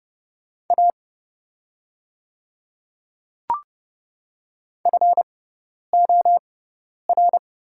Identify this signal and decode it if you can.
Morse code, heard as a keyed tone, about 31 wpm, spaced as for 10 wpm, 700 Hz. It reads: A FOR
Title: 6x Character Spacing - Top 100 Words - 30wpm